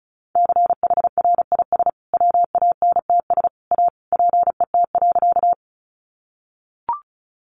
Morse code, heard as a keyed tone, about 34 wpm, 700 Hz. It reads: CHRIS WANTS A PET.
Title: Easy Phrases 35wpm